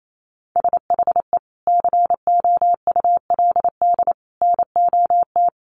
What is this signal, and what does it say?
Morse code, heard as a keyed tone, about 28 wpm, 700 Hz. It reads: SHE COULD NOT
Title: Sentences from Top 1400 Words 28wpm